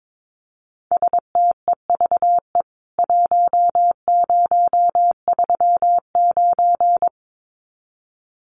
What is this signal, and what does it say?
Morse code, heard as a keyed tone, about 22 wpm, 700 Hz. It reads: STEVE 1039
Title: CWT Exchange 22wpm